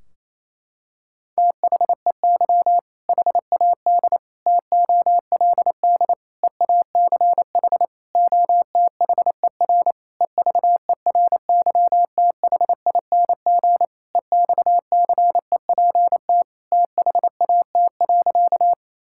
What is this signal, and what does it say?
Morse code, heard as a keyed tone, about 28 wpm, 700 Hz. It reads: THEY HAD TOLD EACH OTHER EVERYTHING EXCEPT THAT.
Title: Sentences from Top 1600 Words 28wpm